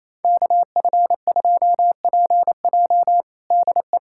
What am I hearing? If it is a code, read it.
Morse code, heard as a keyed tone, about 28 wpm, 700 Hz. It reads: KF2PJ DE
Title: USA Callsign DE Callsign - 5 Characters - 28wpm